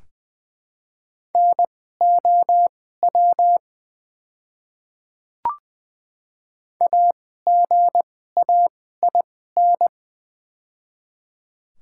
Morse code, heard as a keyed tone, about 20 wpm, spaced as for 14 wpm, 700 Hz. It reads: NOW AGAIN
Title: 2x Character Spacing - Top 300 Words - 20wpm